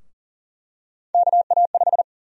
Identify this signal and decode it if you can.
Morse code, heard as a keyed tone, about 40 wpm, 700 Hz. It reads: KA5
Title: USA Call Sign Prefixes - 40wpm